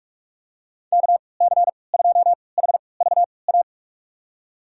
Morse code, heard as a keyed tone, about 46 wpm, spaced as for 25 wpm, 700 Hz. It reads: KC2HVA
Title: USA Call Signs - 6 Characters - 3x Character Spacing - 45wpm